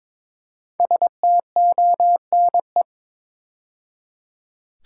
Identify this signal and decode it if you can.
Morse code, heard as a keyed tone, about 23 wpm, 700 Hz. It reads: STONE
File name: 5 Letter Words - First 3 Letters Given - 22wpm